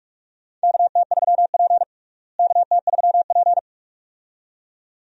Morse code, heard as a keyed tone, about 45 wpm, 700 Hz. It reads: KT3P KT3P
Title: USA Call Signs - 4 Characters - Repeated 2x - 45wpm